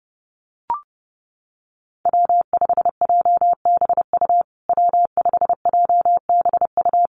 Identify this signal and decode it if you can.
Morse code, heard as a keyed tone, about 30 wpm, 700 Hz. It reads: W5JBU W5JBU
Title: USA Call Signs Repeated 3x 30wpm